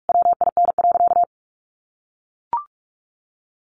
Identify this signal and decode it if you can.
Morse code, heard as a keyed tone, about 44 wpm, 700 Hz. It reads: WIN.
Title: Sentences from Top 1500 Words 45wpm